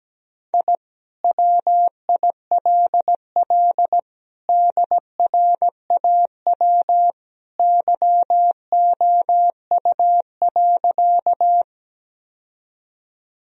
Morse code, heard as a keyed tone, about 17 wpm, 700 Hz. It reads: I WILL DRAW YOU.
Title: Sentences from Top 1000 Words - Review Part 1 - 17wpm